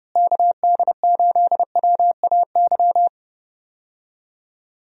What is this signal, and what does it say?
Morse code, heard as a keyed tone, about 30 wpm, 700 Hz. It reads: KD8WAY
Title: USA Call Signs - 6 Characters - Repeated 2x - 30wpm